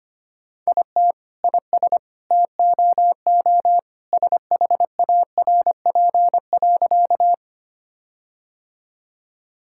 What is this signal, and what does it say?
Morse code, heard as a keyed tone, about 25 wpm, 700 Hz. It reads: IT IS TOO SHARP.
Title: Sentences from Top 1600 Words 25wpm